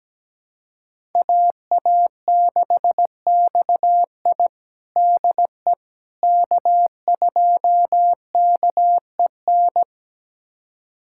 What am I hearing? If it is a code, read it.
Morse code, heard as a keyed tone, about 17 wpm, 700 Hz. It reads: AA6XI DE K2KEN
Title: USA Callsign DE Callsign - 5 Characters - 17wpm